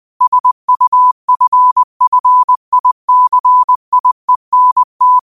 Morse code, heard as a keyed tone, about 20 wpm, 1000 Hz. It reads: SUFFICIENT
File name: qotdfiles / 20wpm